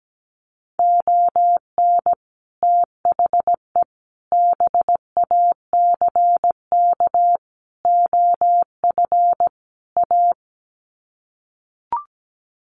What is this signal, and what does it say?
Morse code, heard as a keyed tone, about 17 wpm, 700 Hz. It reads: ON THE BACK OF A
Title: Sets of 5 Words Encore2 17wpm